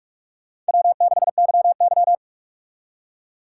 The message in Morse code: W6YY

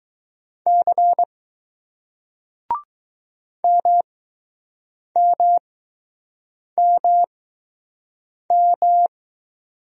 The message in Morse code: C MMMM

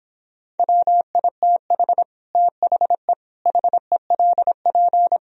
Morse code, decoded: WITH THE HELP

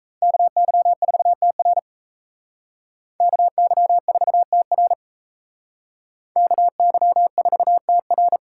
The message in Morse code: KY4TR KY4TR KY4TR

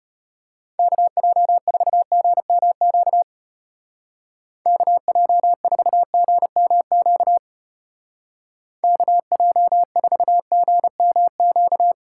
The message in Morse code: KJ4GMQ KJ4GMQ KJ4GMQ

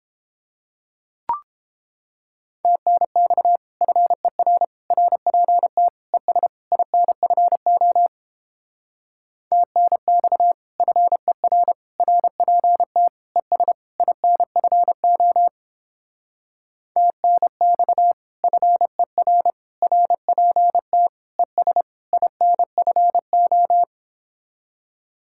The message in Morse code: TNX FER RPT ES INFO TNX FER RPT ES INFO TNX FER RPT ES INFO

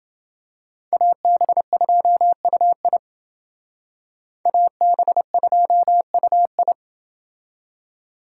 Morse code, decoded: AB2UI AB2UI

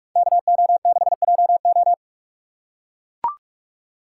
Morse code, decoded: KO6JO

